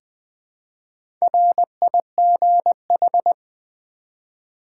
Morse code, heard as a keyed tone, about 20 wpm, 700 Hz. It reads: RIGH
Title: Build A Word - 1 Letter At A Time - 5-Letter Words - 20wpm